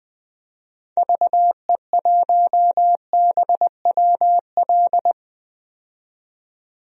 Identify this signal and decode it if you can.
Morse code, heard as a keyed tone, about 21 wpm, 700 Hz. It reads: VE1BWL
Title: DXCC-100 Call Signs - 6 Characters - 20wpm